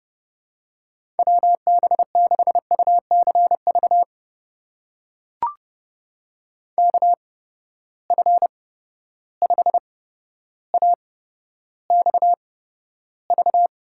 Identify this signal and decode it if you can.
Morse code, heard as a keyed tone, about 32 wpm, spaced as for 13 wpm, 700 Hz. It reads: WB6UCV KF5AXV
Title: USA Call Signs - 6 Characters - 8x Character Spacing - 30wpm